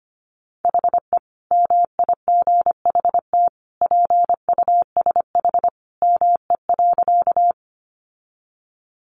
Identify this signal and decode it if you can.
Morse code, heard as a keyed tone, about 25 wpm, 700 Hz. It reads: HE MIGHT PUSH ME.
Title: Sentences from Top 1300 Words 25wpm